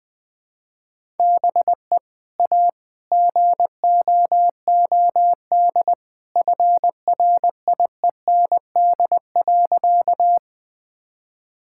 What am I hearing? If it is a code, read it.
Morse code, heard as a keyed tone, about 20 wpm, 700 Hz. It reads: BE A GOOD FRIEND.
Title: Sentences with 2-Word Phrases - 20wpm